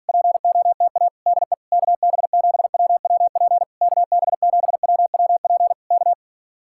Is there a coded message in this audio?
Yes: Morse code from 0.1 to 6.1 s, heard as a keyed tone, about 46 wpm, 700 Hz. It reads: POTA DE KB7WWP KB7WWP K